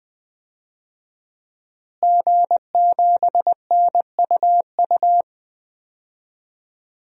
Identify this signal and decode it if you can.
Morse code, heard as a keyed tone, about 20 wpm, 700 Hz. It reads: G7NUU
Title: DXCC-100 Call Signs - 5 Characters - Repeated 2x - 20wpm